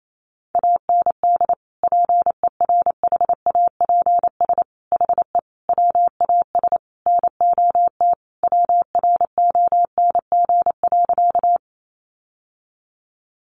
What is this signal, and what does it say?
Morse code, heard as a keyed tone, about 28 wpm, 700 Hz. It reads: AND PERHAPS HE WAS NOT WRONG.
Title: Sentences from Top 900 Words 28wpm